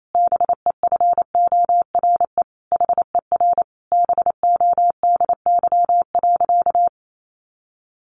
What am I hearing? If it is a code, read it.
Morse code, heard as a keyed tone, about 28 wpm, 700 Hz. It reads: BEFORE HER BODY.
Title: Sentences from Top 1500 Words 28wpm